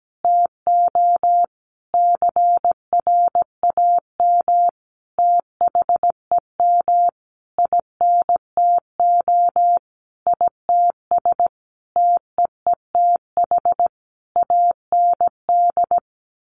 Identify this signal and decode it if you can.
Morse code, heard as a keyed tone, about 17 wpm, 700 Hz. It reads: TO CRAM THEM INTO ITS TEETH AND
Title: A Sound Of Thunder 17wpm